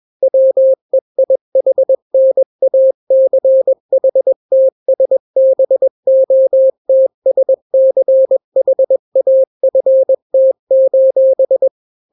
Morse code, heard as a keyed tone, about 21 wpm, 530 Hz. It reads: WEIHNACHTSBOTSCHAFT:
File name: daily / 20wpm